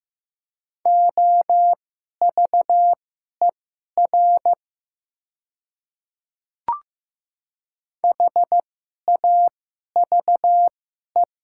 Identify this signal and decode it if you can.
Morse code, heard as a keyed tone, about 15 wpm, spaced as for 11 wpm, 700 Hz. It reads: OVER HAVE
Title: 2x Character Spacing - Top 100 Words - 15wpm